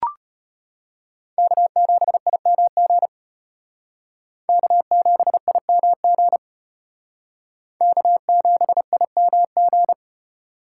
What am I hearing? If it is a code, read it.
Morse code, heard as a keyed tone, about 32 wpm, 700 Hz. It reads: K7IMG K7IMG K7IMG